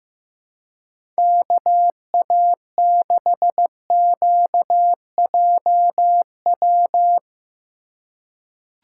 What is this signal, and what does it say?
Morse code, heard as a keyed tone, about 16 wpm, 700 Hz. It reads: KA6QJW
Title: USA Call Signs - 6 Characters - 3x Character Spacing - 15wpm